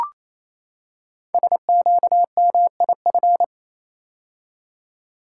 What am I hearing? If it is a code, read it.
Morse code, heard as a keyed tone, about 29 wpm, 700 Hz. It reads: SQMIF
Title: Code Groups - 5 Letters - 28wpm